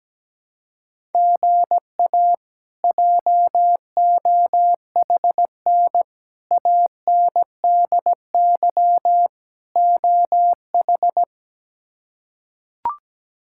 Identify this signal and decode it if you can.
Morse code, heard as a keyed tone, about 17 wpm, 700 Hz. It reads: GA JOHN ANDY OH
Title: K1USN SST QSO Practice - 17wpm